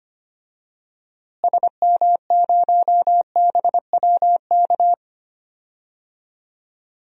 Morse code, heard as a keyed tone, about 25 wpm, 700 Hz. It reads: SM0BWK
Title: DXCC-100 Call Signs - 6 Characters - Repeated 3x - 25wpm